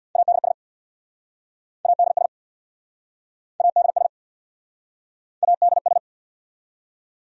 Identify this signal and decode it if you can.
Morse code, heard as a keyed tone, about 69 wpm, 700 Hz. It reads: ADS ADS ADS ADS